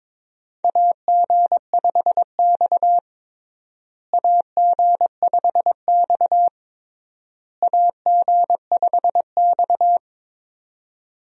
Morse code, heard as a keyed tone, about 22 wpm, 700 Hz. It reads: AG5X AG5X AG5X
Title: USA Call Signs - 4 Characters - Repeated 3x - 22wpm